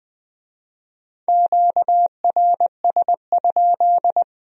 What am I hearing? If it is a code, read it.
Morse code, heard as a keyed tone, about 23 wpm, 700 Hz. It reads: QRS?
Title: Speed-Racing Q-Codes - 20wpm